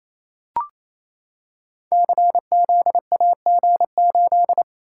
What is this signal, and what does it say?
Morse code, heard as a keyed tone, about 28 wpm, 700 Hz. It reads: CZAG8
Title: Code Groups - 5 Letter:Numbers - 28wpm